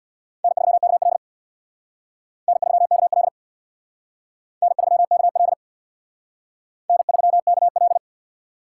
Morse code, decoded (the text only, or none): N3KL N3KL N3KL N3KL